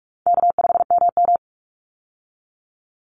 Morse code, heard as a keyed tone, about 44 wpm, 700 Hz. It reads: K5MM
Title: Easy Call Signs Repeated 2x - 45wpm